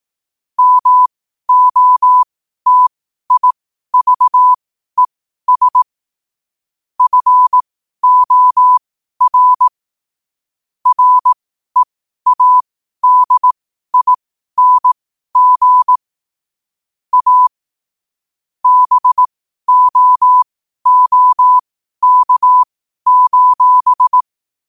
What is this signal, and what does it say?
Morse code, heard as a keyed tone, about 18 wpm, spaced as for 12 wpm, 1000 Hz. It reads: MOTIVES FOR READING A BOOK:
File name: qotdfiles / 12wpm